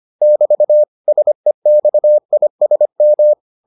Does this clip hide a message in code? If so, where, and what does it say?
Yes: Morse code from 0.2 to 3.3 s, heard as a keyed tone, about 25 wpm, 600 Hz. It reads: <BT> SEXISM